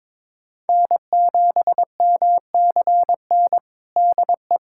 Morse code, heard as a keyed tone, about 21 wpm, 700 Hz. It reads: N7MCN DE
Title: USA Callsign DE Callsign - 5 Characters - 22wpm